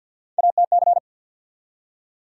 Morse code, heard as a keyed tone, about 50 wpm, 700 Hz. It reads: ATC
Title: USA Call Sign Postfixes - 3 Letters 50wpm